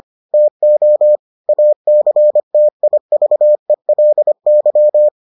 Morse code, heard as a keyed tone, about 25 wpm, 600 Hz. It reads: TO ACTIVELY